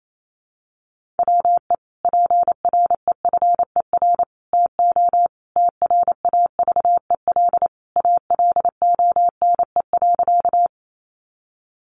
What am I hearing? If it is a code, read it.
Morse code, heard as a keyed tone, about 28 wpm, 700 Hz. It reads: WE PREFER TO TRAVEL ALONE.
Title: Sentences from Top 1100 Words 28wpm